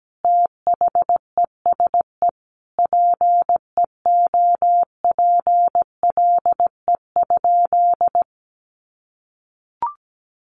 Morse code, heard as a keyed tone, about 17 wpm, 700 Hz. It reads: THESE PEOPLE?
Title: Sentences from Top 200 Words 17wpm